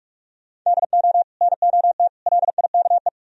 Morse code, heard as a keyed tone, about 54 wpm, 700 Hz. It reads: DO NOT LIKE
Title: Speed-Racing - Sets of 3 Words 45wpm